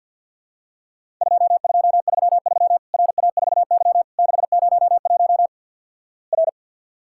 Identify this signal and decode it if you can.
Morse code, heard as a keyed tone, about 50 wpm, 700 Hz. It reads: 2233 RA4Y 601 R
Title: Worked All Europe DX Contest - QTC Exchange with Repeats - 50wpm